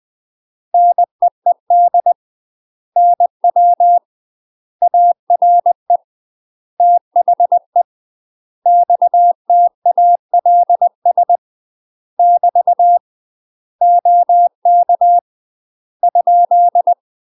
Morse code, heard as a keyed tone, about 20 wpm, 700 Hz. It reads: NEED NW ARE THE XTALS <BT> OK ?
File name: UK MASTER QSOs TRAINING / 20WPM / UK MASTER QSOs 0047